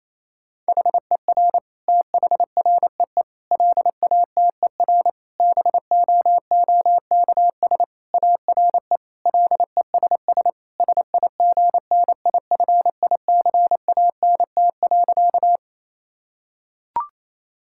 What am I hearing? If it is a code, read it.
Morse code, heard as a keyed tone, about 28 wpm, 700 Hz. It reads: HER THREE LATER BOOKS ARE LESS SIGNIFICANT.
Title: Sentences from Top 1000 Words - Review Part 6 - 28wpm